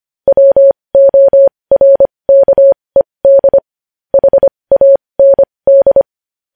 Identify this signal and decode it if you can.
Morse code, heard as a keyed tone, about 25 wpm, spaced as for 20 wpm, 550 Hz. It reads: WORKED HAND
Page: News Headlines in Morse Code at 20 WPM